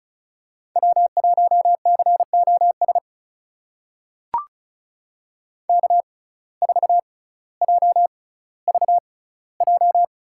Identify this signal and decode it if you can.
Morse code, heard as a keyed tone, about 34 wpm, spaced as for 18 wpm, 700 Hz. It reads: W1COS K4JVJ
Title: USA Call Signs - 5 Characters - 6x Character Spacing - 35wpm